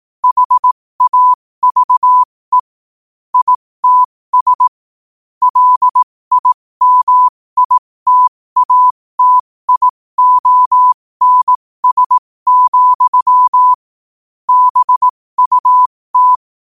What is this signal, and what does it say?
Morse code, heard as a keyed tone, about 18 wpm, spaced as for 15 wpm, 1000 Hz. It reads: HAVE ITS LIMITATIONS, BUT